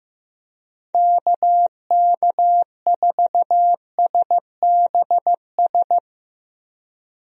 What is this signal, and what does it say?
Morse code, heard as a keyed tone, about 15 wpm, 700 Hz. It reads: KK4SBS
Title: USA Call Signs - 6 Characters - Repeated 3x - 15wpm